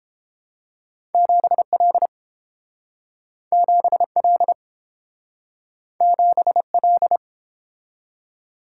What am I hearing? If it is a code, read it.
Morse code, heard as a keyed tone, about 30 wpm, 700 Hz. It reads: 7L 7L 7L